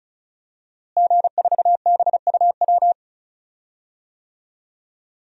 Morse code, heard as a keyed tone, about 36 wpm, 700 Hz. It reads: G4BUW
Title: DXCC-100 Call Signs - 5 Characters - Repeated 3x - 35wpm